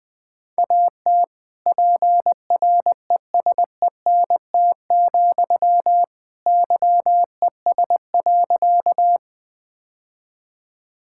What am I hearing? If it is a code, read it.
Morse code, heard as a keyed tone, about 20 wpm, 700 Hz. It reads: AT PRESENT, YES.
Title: Sentences from Top 1000 Words - Review Part 5 - 20wpm